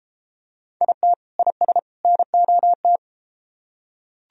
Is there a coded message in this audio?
Yes: Morse code from 0.8 to 3.0 s, heard as a keyed tone, about 32 wpm, 700 Hz. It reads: IT IS NOT